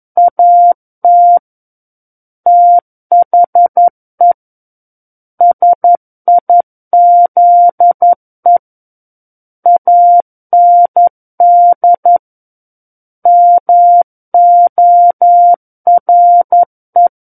Morse code, heard as a keyed tone, about 11 wpm, 700 Hz. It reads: AT THE SIZE AND MORE
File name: Book One / 02-Chapter 2 - 11 WPM